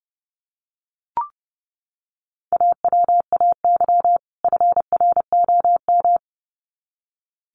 Morse code, heard as a keyed tone, about 30 wpm, 700 Hz. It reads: AWAY FROM
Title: Sets of 2 Words 30wpm